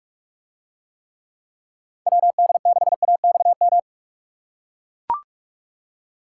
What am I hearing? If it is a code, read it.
Morse code, heard as a keyed tone, about 44 wpm, 700 Hz. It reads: WD6AXM